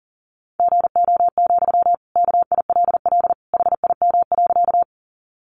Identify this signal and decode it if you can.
Morse code, heard as a keyed tone, about 40 wpm, 700 Hz. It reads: GO, KILL HIM.
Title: Sentences from Top 1400 Words 40wpm